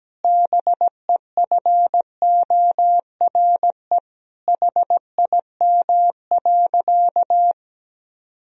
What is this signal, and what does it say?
Morse code, heard as a keyed tone, about 17 wpm, 700 Hz. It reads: BEFORE HIM.
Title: Sentences from Top 1000 Words - Review Part 2 - 17wpm